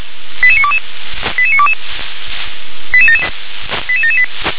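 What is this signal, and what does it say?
Five-tone selective call (ZVEI), heard as quick runs of five tones, tones of about 70 ms. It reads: 88922 88922 88778
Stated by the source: ZVEI five-tone call received by SDR.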